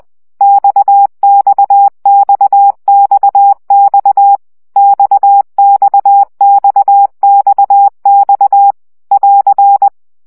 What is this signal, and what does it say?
Morse code, heard as a keyed tone, about 20 wpm, 800 Hz. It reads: XXXXX XXXXX <AR>
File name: CW gyakorlas x 17 baud